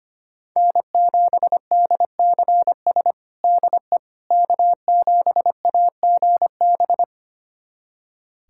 Morse code, heard as a keyed tone, about 25 wpm, 700 Hz. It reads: N7DCS DE K7AGB
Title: USA Callsign DE Callsign 25wpm